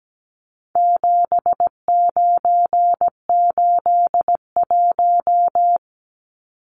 Morse code, heard as a keyed tone, about 17 wpm, 700 Hz. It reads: 7981
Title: Numbers - 4-Digit 17wpm